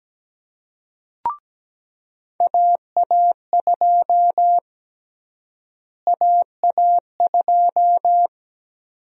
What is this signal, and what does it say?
Morse code, heard as a keyed tone, about 17 wpm, 700 Hz. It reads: AA2 AA2
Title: USA Call Sign Prefixes - Repeated 2x - 17wpm